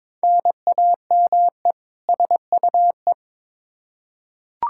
Morse code, heard as a keyed tone, about 22 wpm, 700 Hz. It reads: NAME SUE
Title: Practice QSO - Exchange 1 - Beginning - 22wpm